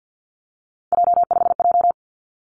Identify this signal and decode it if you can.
Morse code, heard as a keyed tone, about 50 wpm, 700 Hz. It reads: J5J
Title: Three Challenging Letter-Number Combinations 50wpm